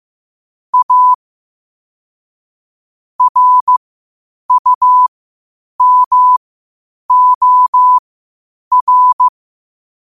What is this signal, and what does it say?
Morse code, heard as a keyed tone, about 15 wpm, spaced as for 8 wpm, 1000 Hz. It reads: A RUMOR
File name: 8wpm